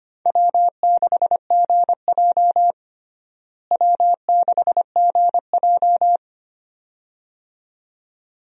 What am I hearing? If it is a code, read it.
Morse code, heard as a keyed tone, about 25 wpm, 700 Hz. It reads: W6GJ W6GJ
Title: USA Call Signs - 4 Characters - Repeated 2x - 25wpm